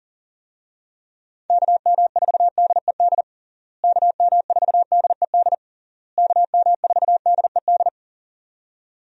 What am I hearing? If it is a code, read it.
Morse code, heard as a keyed tone, about 40 wpm, 700 Hz. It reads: KM4DED KM4DED KM4DED
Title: USA Call Signs - 6 Characters - Repeated 3x - 40wpm